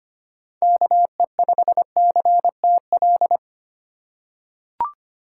Morse code, heard as a keyed tone, about 25 wpm, 700 Hz. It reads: KE5CTL
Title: USA Call Signs - 6 Characters - 2x Character Spacing - 25wpm